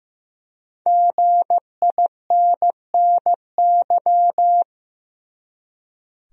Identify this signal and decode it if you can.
Morse code, heard as a keyed tone, about 15 wpm, 700 Hz. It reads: GINNY
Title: USA Common Names - 15wpm